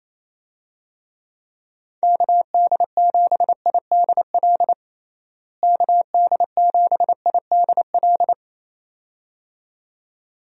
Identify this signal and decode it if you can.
Morse code, heard as a keyed tone, about 28 wpm, 700 Hz. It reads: KD7IDL KD7IDL
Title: USA Call Signs - 6 Characters - Repeated 2x - 28wpm